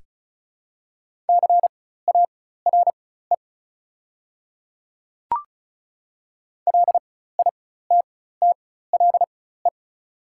Morse code, heard as a keyed tone, about 36 wpm, spaced as for 16 wpm, 690 Hz. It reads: CARE LITTLE
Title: 4x Character Spacing - Top 300 Words - 35wpm